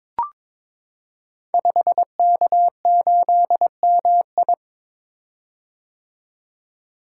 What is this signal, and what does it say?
Morse code, heard as a keyed tone, about 22 wpm, 700 Hz. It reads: 5K8MI
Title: Code Groups - 5 Letter:Numbers - 22wpm